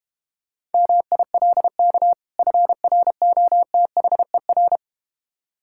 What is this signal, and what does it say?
Morse code, heard as a keyed tone, about 32 wpm, 700 Hz. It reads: MILK FROTHER